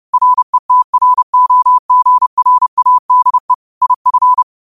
Morse code, heard as a keyed tone, about 30 wpm, 1000 Hz. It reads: RETROGRADE IF
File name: qotdfiles / 30wpm